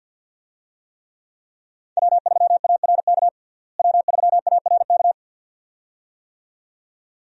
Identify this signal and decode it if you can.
Morse code, heard as a keyed tone, about 52 wpm, 700 Hz. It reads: W3ARK W3ARK
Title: USA Call Signs - 5 Characters - Repeated 2x - 50wpm